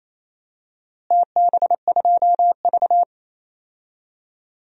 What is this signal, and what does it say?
Morse code, heard as a keyed tone, about 28 wpm, 700 Hz. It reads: TB2V
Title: DXCC-100 Call Signs - 4 Characters - Repeated 3x - 28wpm